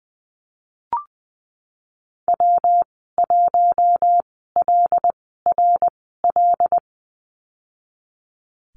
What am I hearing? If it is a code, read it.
Morse code, heard as a keyed tone, about 20 wpm, spaced as for 14 wpm, 700 Hz. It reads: W1LRL
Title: 122 Lesson 20g - Number 3 - Call Signs 20wpm